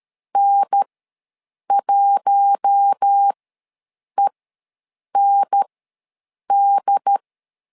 Morse code, heard as a keyed tone, about 13 wpm, spaced as for 7 wpm, 800 Hz. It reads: N1END
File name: LEZIONI / 4 Q1 ws 12wpm 2129